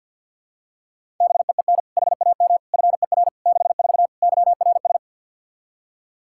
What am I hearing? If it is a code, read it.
Morse code, heard as a keyed tone, about 51 wpm, 700 Hz. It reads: BEEN HAM FER 64 YRS
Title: K1USN SST QSO Practice 50wpm